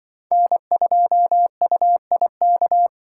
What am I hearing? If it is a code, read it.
Morse code, heard as a keyed tone, about 23 wpm, 700 Hz. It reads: N2UIK
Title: Speed Racing - USA 1x3 Call Signs 20wpm